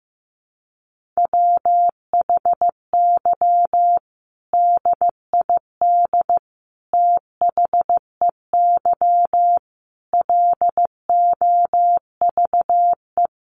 Morse code, heard as a keyed tone, about 15 wpm, 700 Hz. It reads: WHY DID THEY LOVE
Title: Sentences from Top 1500 Words 15wpm